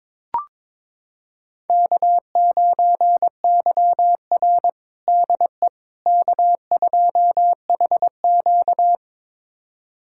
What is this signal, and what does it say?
Morse code, heard as a keyed tone, about 22 wpm, 700 Hz. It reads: K9YR DE K2HQ
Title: USA Callsign DE Callsign - 4 Characters - 22wpm